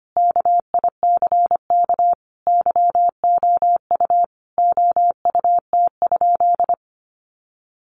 Morse code, heard as a keyed tone, about 25 wpm, 700 Hz. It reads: KICK YOU OUT?
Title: Sentences from Top 1300 Words 25wpm